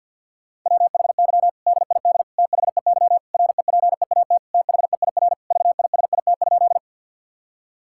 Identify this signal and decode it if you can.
Morse code, heard as a keyed tone, about 50 wpm, 700 Hz. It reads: WHY DID THEY REPEAT THEIR VISIT?
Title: Sentences from Top 1600 Words 50wpm